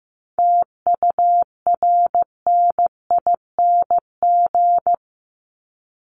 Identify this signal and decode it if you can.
Morse code, heard as a keyed tone, about 15 wpm, 700 Hz. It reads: TURNING
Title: ING Suffix - Top 100 Words with Base Words 15wpm